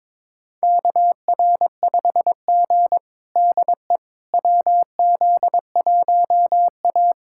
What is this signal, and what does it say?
Morse code, heard as a keyed tone, about 22 wpm, 700 Hz. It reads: KR5G DE WZ1A
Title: USA Callsign DE Callsign - 4 Characters - 22wpm